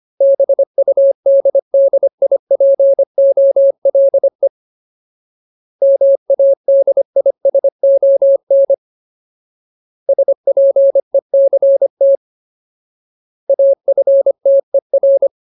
Morse code, heard as a keyed tone, about 25 wpm, 550 Hz. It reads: BUDDIPOLE MADISON SPECT AFTER
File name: Ultimate Word List / 25WPM Ultimate